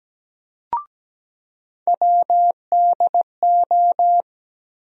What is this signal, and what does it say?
Morse code, heard as a keyed tone, about 17 wpm, 700 Hz. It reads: WDO
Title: USA Call Sign Postfixes - 3 Letters 17wpm